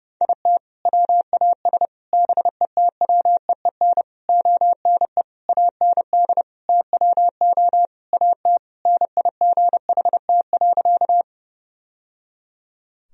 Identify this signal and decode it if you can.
Morse code, heard as a keyed tone, about 32 wpm, 700 Hz. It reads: IT WAS BETWEEN ONE AND TWO AT NIGHT.
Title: Sentences from Top 1000 Words - Review Part 1 - 30wpm